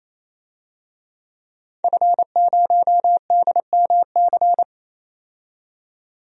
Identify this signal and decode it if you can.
Morse code, heard as a keyed tone, about 28 wpm, 700 Hz. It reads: F0DMC
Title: DXCC-100 Call Signs - 5 Characters - 28wpm